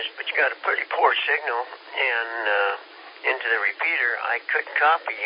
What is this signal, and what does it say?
Emergency Alert System (SAME) tones, heard as harsh buzzing bursts, with speech. It is heard from 0.2 to 4.2 s.